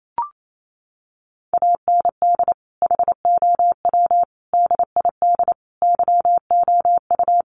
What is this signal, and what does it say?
Morse code, heard as a keyed tone, about 28 wpm, 700 Hz. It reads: AND HOW DID YOU
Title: Sentences from Top 1400 Words 28wpm